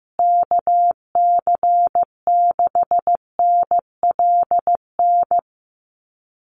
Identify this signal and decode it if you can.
Morse code, heard as a keyed tone, about 14 wpm, 700 Hz. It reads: KC6NLN
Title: USA Call Signs Repeated 3x 15wpm